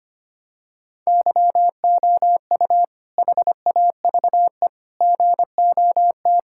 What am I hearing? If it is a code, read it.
Morse code, heard as a keyed tone, about 25 wpm, 700 Hz. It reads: YOU HAVE GOT
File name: practice / 25wpm / Sentences with 2-Word Phrases - 25wpm